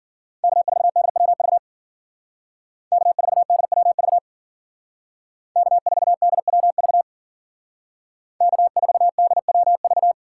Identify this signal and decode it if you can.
Morse code, heard as a keyed tone, about 50 wpm, 700 Hz. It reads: K4DWV K4DWV K4DWV K4DWV
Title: Speed Racing - USA 1x3 Call Signs 40wpm